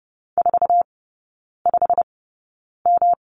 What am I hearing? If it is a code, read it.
Morse code, heard as a keyed tone, about 30 wpm, spaced as for 9 wpm, 700 Hz. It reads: 45M
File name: Lesson 19e - Letter w - Three Characters 30wpm